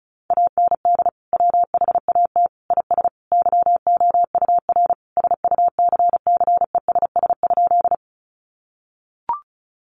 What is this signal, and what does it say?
Morse code, heard as a keyed tone, about 34 wpm, 700 Hz. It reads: AND WHAT IS YOUR SUCCESS?
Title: Sentences from Top 900 Words 35wpm